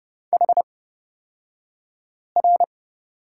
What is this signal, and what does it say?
Morse code, heard as a keyed tone, about 30 wpm, spaced as for 5 wpm, 700 Hz. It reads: HR
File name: Lesson 38d - Letter q - Two Characters 30wpm